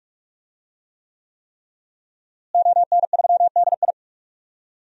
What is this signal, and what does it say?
Morse code, heard as a keyed tone, about 42 wpm, 700 Hz. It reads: ON3DI
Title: DXCC-100 Call Signs - 5 Characters - Repeated 3x - 45wpm